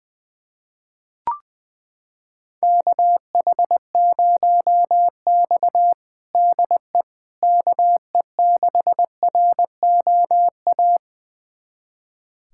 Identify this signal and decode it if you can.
Morse code, heard as a keyed tone, about 20 wpm, 700 Hz. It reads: KH0X DE KE6ROA
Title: USA Callsign DE Callsign 20wpm